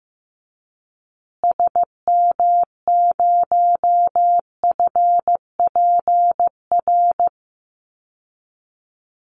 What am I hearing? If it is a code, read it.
Morse code, heard as a keyed tone, about 15 wpm, 700 Hz. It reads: SM0FPR
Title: Call Signs Repeated 2x - 15wpm